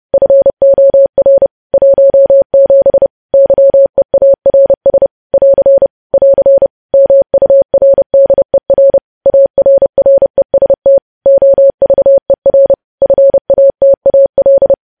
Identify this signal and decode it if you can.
Morse code, heard as a keyed tone, about 30 wpm, 550 Hz. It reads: FOR 17 YEARS <AR> <AR> MURDER ARREST OVER FATAL
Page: news headlines in morse code at 30 wpm